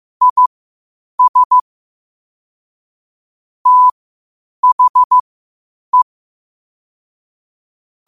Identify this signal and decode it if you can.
Morse code, heard as a keyed tone, about 15 wpm, spaced as for 8 wpm, 1000 Hz. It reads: IS THE
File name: qotdfiles / 8wpm